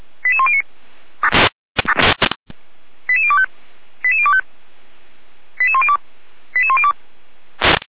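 Five-tone selective call (ZVEI), heard as quick runs of five tones, tones of about 70 ms. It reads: 88109 88025 88025 88102 88102
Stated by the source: ZVEI five-tone call received on SDR, logged by Feuerwehr Orlamünde.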